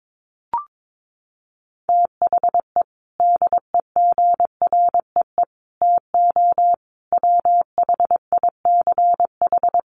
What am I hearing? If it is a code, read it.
Morse code, heard as a keyed tone, about 22 wpm, 700 Hz. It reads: THE DEGREE TO WHICH
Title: Sets of 5 Words Encore2 22wpm